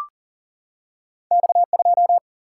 Morse code, heard as a keyed tone, about 39 wpm, 700 Hz. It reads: X2
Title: Single-Letter Single-Number 40wpm